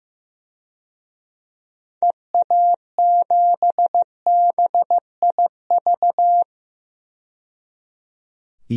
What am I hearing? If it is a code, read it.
Morse code, heard as a keyed tone, about 15 wpm, 690 Hz. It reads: EA7BIV
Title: DXCC-100 Call Signs - 6 Characters - 15wpm